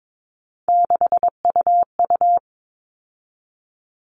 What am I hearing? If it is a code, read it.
Morse code, heard as a keyed tone, about 22 wpm, 700 Hz. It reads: 6UU
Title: All Cut Numbers - 3-Digit 22wpm